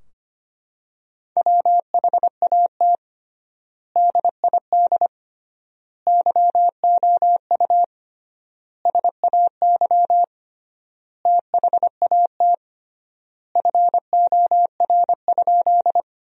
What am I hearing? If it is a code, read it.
Morse code, heard as a keyed tone, about 25 wpm, 700 Hz. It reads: WHAT DID YOU SAY THAT FOR?
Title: 3x Word-Spacing - Sentences from Top 200 Words - 25wpm